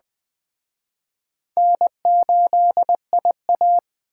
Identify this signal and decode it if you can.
Morse code, heard as a keyed tone, about 21 wpm, 700 Hz. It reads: N8IA
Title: Speed Racing - USA 1x2 Call Signs 17wpm